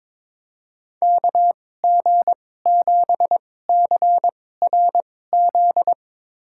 Morse code, heard as a keyed tone, about 21 wpm, spaced as for 16 wpm, 700 Hz. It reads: KG7CRZ
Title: USA Call Signs - 6 Characters - 2x Character Spacing - 22wpm